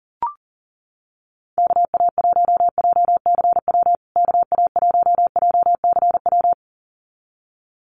Morse code, heard as a keyed tone, about 37 wpm, 700 Hz. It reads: KA1JCW KA1JCW
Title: USA Call Signs Repeated 2x 40wpm